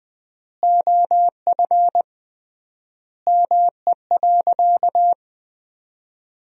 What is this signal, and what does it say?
Morse code, heard as a keyed tone, about 19 wpm, 700 Hz. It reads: OF ME.
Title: 3x Word-Spacing - Sentences from Top 300 Words - 20wpm